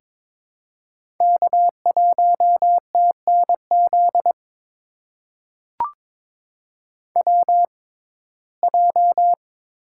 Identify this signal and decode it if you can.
Morse code, heard as a keyed tone, about 22 wpm, spaced as for 16 wpm, 700 Hz. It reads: K1TNZ WJ
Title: USA Call Signs - 6x Character Spacing - 22wpm